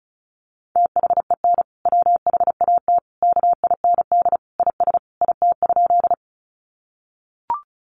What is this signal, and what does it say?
Morse code, heard as a keyed tone, about 34 wpm, 700 Hz. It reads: THEN WHAT KIND IS IT?
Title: Sentences from Top 200 Words 35wpm